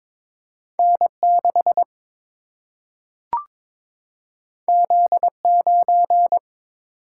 Morse code, heard as a keyed tone, about 22 wpm, 700 Hz. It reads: N 6 Z 9
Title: Single-Letter Single-Number 22wpm